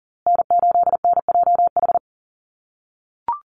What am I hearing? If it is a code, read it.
Morse code, heard as a keyed tone, about 40 wpm, 700 Hz. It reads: N8NJH
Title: USA Call Signs Repeated 3x 40wpm